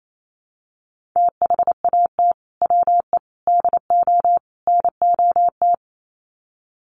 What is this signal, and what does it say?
Morse code, heard as a keyed tone, about 28 wpm, 700 Hz. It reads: THAT WE DO NOT
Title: Sets of 4 Words 28wpm